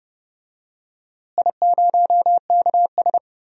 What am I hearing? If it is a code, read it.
Morse code, heard as a keyed tone, about 29 wpm, 700 Hz. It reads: I0KS
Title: DXCC-100 Call Signs - 4 Characters - Repeated 2x - 30wpm